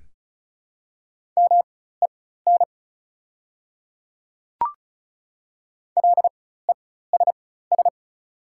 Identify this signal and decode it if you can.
Morse code, heard as a keyed tone, about 34 wpm, spaced as for 16 wpm, 690 Hz. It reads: MEN LESS
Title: 4x Character Spacing - Top 500 Words - 35wpm